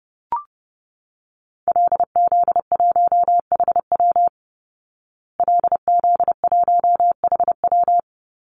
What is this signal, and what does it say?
Morse code, heard as a keyed tone, about 30 wpm, 700 Hz. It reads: LZ1HW LZ1HW
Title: Call Signs Repeated 2x - 30wpm